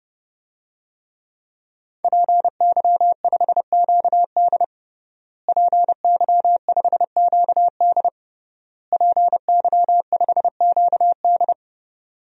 Morse code, heard as a keyed tone, about 30 wpm, 700 Hz. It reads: PY5QD PY5QD PY5QD
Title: DXCC-100 Call Signs - 5 Characters - Repeated 3x - 30wpm